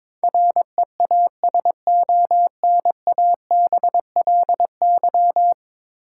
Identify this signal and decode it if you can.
Morse code, heard as a keyed tone, about 22 wpm, 700 Hz. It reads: REASONABLY
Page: morse-code-ninja-practice-22wpm